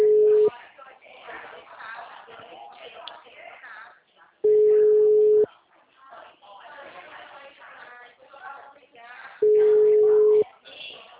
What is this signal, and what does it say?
Ringback tone (Europe, 425 Hz), with speech.